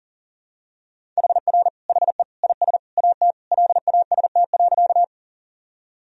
Morse code, heard as a keyed tone, about 40 wpm, 700 Hz. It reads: HR HE IS AT LAST.
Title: Sentences from Top 300 Words and Abbreviations 40wpm